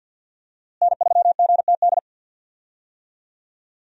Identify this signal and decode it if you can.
Morse code, heard as a keyed tone, about 50 wpm, 700 Hz. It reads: N3GTD
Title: USA Call Signs - 5 Characters - Repeated 2x - 50wpm